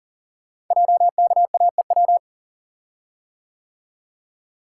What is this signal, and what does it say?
Morse code, heard as a keyed tone, about 41 wpm, 690 Hz. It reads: JKAEW
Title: Code Groups - 5 Letters - 40wpm